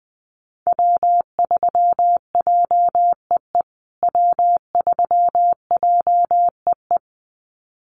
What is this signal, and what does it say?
Morse code, heard as a keyed tone, about 20 wpm, 700 Hz. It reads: W3JEE W3JEE
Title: USA Call Signs Repeated 2x 20wpm